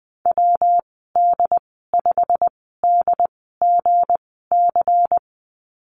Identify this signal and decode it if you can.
Morse code, heard as a keyed tone, about 20 wpm, spaced as for 15 wpm, 700 Hz. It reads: WD5DGC
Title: 207 Lesson 33g - Letter j - Call Signs 20wpm